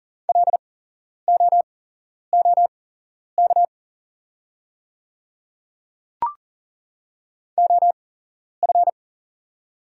Morse code, heard as a keyed tone, about 40 wpm, spaced as for 11 wpm, 690 Hz. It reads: LOOK OF
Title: 8x Character Spacing - Top 100 Words - 40wpm